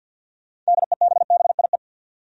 Morse code, heard as a keyed tone, about 53 wpm, 700 Hz. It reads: DEBBIE
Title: USA Common Names - 50wpm